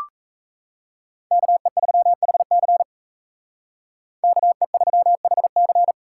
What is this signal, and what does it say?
Morse code, heard as a keyed tone, about 32 wpm, 700 Hz. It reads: KE3HC KE3HC
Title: Speed Racing - USA 2x2 Call Signs 28wpm